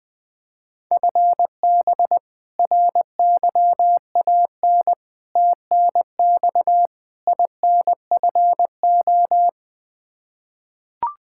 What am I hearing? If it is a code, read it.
Morse code, heard as a keyed tone, about 20 wpm, 700 Hz. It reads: FB RYAN TNX INFO
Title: Practice QSO - Exchange 3 - More Info - 20wpm